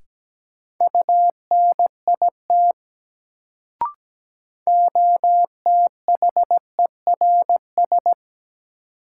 Morse code, heard as a keyed tone, about 17 wpm, 700 Hz. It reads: UNIT OTHERS
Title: Sentences from Top 1000 Words - Review Part 2 - New Word Drill - 17wpm